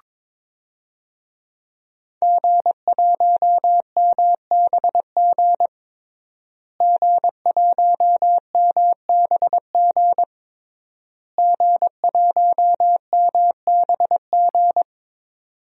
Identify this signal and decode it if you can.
Morse code, heard as a keyed tone, about 22 wpm, 700 Hz. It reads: G1MBG G1MBG G1MBG
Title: DXCC-100 Call Signs - 5 Characters - Repeated 3x - 22wpm